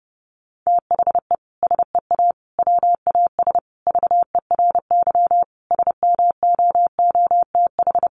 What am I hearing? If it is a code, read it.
Morse code, heard as a keyed tone, about 30 wpm, 700 Hz. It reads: THE SEA WAS VERY SMOOTH
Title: Sentences from Top 1400 Words 30wpm